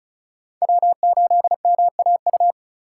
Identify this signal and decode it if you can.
Morse code, heard as a keyed tone, about 36 wpm, 700 Hz. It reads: W8MAU
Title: USA Call Signs - 5 Characters - 2x Character Spacing - 35wpm